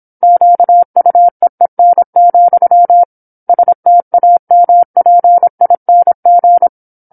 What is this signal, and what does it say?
Morse code, heard as a keyed tone, about 27 wpm, 700 Hz. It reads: QUEEN, STAMPING
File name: 08-Chapter-08 26 WPM